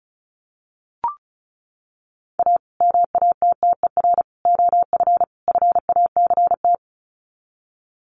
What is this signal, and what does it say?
Morse code, heard as a keyed tone, about 34 wpm, 700 Hz. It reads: A MATTER OF FACT
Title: Sets of 4 Words 35wpm